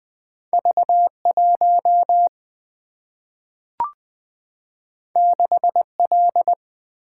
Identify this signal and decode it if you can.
Morse code, heard as a keyed tone, about 20 wpm, 700 Hz. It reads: V1 6L
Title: Single-Letter Single-Number 20wpm